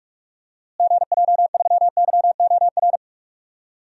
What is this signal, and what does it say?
Morse code, heard as a keyed tone, about 46 wpm, 700 Hz. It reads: GJ3YOR
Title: DXCC-100 Call Signs - 6 Characters - Repeated 3x - 45wpm